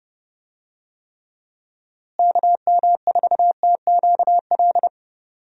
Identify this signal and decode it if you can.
Morse code, heard as a keyed tone, about 30 wpm, 700 Hz. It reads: KM4TQL
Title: USA Call Signs - 6 Characters - Repeated 3x - 30wpm